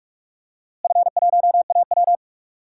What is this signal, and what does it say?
Morse code, heard as a keyed tone, about 46 wpm, 700 Hz. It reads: F1AW
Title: DXCC-100 Call Signs - 4 Characters - Repeated 2x - 45wpm